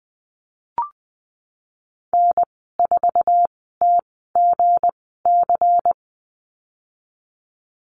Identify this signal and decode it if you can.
Morse code, heard as a keyed tone, about 20 wpm, spaced as for 15 wpm, 700 Hz. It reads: N4TGC
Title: 186 Lesson 30g - Letter b - Call Signs 20wpm